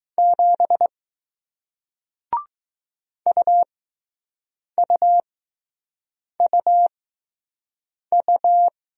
Morse code, heard as a keyed tone, about 20 wpm, spaced as for 6 wpm, 700 Hz. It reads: 7 UUUU